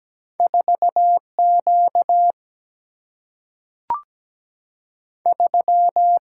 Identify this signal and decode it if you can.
Morse code, heard as a keyed tone, about 17 wpm, 700 Hz. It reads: 4 Q 3
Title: Single-Letter Single-Number 17wpm